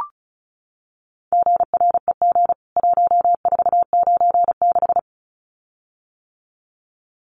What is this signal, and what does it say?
Morse code, heard as a keyed tone, about 34 wpm, 700 Hz. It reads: GREG 1496
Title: CWT Exchange 35wpm